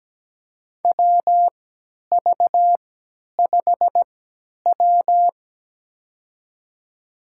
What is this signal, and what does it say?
Morse code, heard as a keyed tone, about 17 wpm, spaced as for 10 wpm, 700 Hz. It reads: WV5W
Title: USA Call Signs - 3x Character Spacing - 17wpm